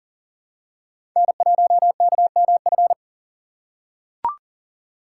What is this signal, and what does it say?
Morse code, heard as a keyed tone, about 40 wpm, 700 Hz. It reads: N1KMF
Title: USA Call Signs - 5 Characters - 3x Character Spacing - 40wpm